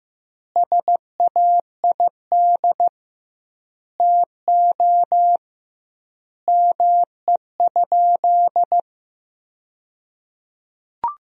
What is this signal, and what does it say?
Morse code, heard as a keyed tone, about 15 wpm, 700 Hz. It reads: SAID TO ME?
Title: 2x Word-Spacing - Sentences from Top 200 Words - 15wpm